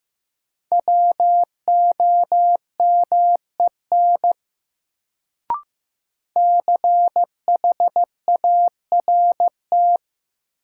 Morse code, heard as a keyed tone, about 16 wpm, 700 Hz. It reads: WOMEN CHART
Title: Sentences from Top 1300 Words - New Word Drill Alt - 15wpm